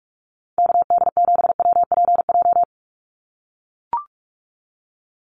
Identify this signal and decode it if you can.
Morse code, heard as a keyed tone, about 44 wpm, 700 Hz. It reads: KD7WPJ
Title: NA SOTA Activator Callsigns 45wpm